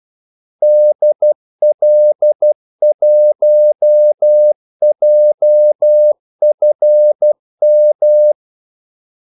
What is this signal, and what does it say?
Morse code, heard as a keyed tone, about 12 wpm, 600 Hz. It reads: DL1JFM
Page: 2 ekstra ordmellemrum mellem kaldesignalerne